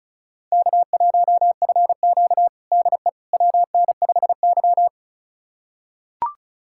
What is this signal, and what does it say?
Morse code, heard as a keyed tone, about 36 wpm, 700 Hz. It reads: K1FQ DE WN5Y
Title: USA Callsign DE Callsign - 4 Characters - 35wpm